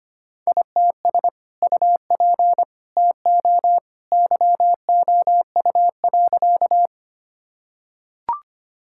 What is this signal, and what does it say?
Morse code, heard as a keyed tone, about 25 wpm, 700 Hz. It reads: ITS UP TO YOU.
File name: practice / 25wpm / Sentences with 2-Word Phrases - 25wpm